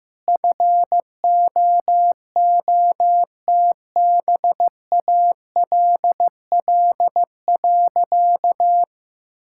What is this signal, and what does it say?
Morse code, heard as a keyed tone, about 14 wpm, 700 Hz. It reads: FOOTBALL.
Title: Sentences from Top 1600 Words 15wpm